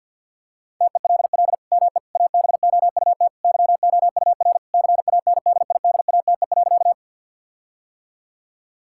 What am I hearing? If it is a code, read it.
Morse code, heard as a keyed tone, about 52 wpm, 700 Hz. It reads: TELL ME ABOUT YOUR CANDIDATE.
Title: Sentences from Top 1600 Words 50wpm